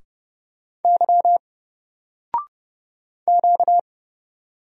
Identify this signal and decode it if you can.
Morse code, heard as a keyed tone, about 30 wpm, 700 Hz. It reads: Y Q
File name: Single-Letter-Number-30wpm